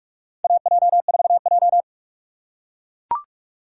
Morse code, heard as a keyed tone, about 42 wpm, 700 Hz. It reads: AJ4J
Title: USA Call Signs - 4 Characters - 6x Character Spacing - 45wpm